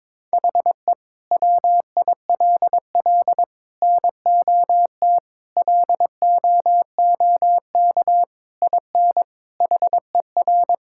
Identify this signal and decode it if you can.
Morse code, heard as a keyed tone, about 22 wpm, 700 Hz. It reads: HE WILL NOT LOOK IN HER
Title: Sentences from Top 1000 Words - Review Part 5 - 22wpm